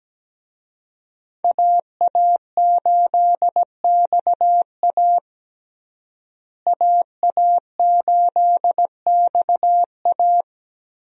Morse code, heard as a keyed tone, about 17 wpm, 700 Hz. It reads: AA8XA AA8XA
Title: USA Call Signs - 5 Characters - Repeated 3x - 17wpm